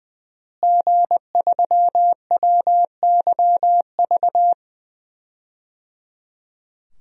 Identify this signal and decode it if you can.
Morse code, heard as a keyed tone, about 20 wpm, 700 Hz. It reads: G3WYV
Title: DXCC-100 Call Signs - 5 Characters - Repeated 3x - 20wpm